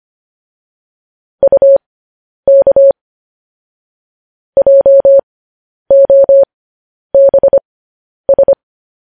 Morse code, heard as a keyed tone, about 25 wpm, spaced as for 10 wpm, 550 Hz. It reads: UK JOBS
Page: News Headlines in Morse Code at 10 WPM